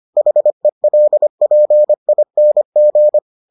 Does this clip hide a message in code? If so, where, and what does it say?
Yes: Morse code from 0.2 to 3.2 s, heard as a keyed tone, about 25 wpm, 600 Hz. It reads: HELPING